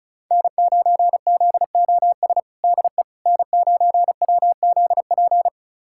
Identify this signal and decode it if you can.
Morse code, heard as a keyed tone, about 36 wpm, 700 Hz. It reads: N9ZOS DE N9WZP
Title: USA Callsign DE Callsign - 5 Characters - 35wpm